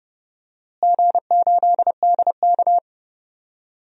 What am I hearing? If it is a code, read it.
Morse code, heard as a keyed tone, about 31 wpm, 700 Hz. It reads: G8DK
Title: DXCC-100 Call Signs - 4 Characters - Repeated 3x - 30wpm